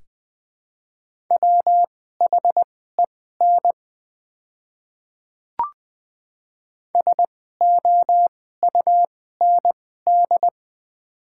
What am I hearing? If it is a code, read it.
Morse code, heard as a keyed tone, about 20 wpm, spaced as for 14 wpm, 690 Hz. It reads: WHEN SOUND
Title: 2x Character Spacing - Top 100 Words - 20wpm